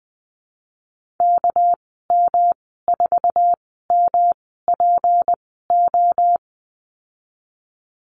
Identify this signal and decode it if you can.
Morse code, heard as a keyed tone, about 20 wpm, spaced as for 14 wpm, 700 Hz. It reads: KM4MPO
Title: 213 Lesson 34g - Number 8 - Call Signs 20wpm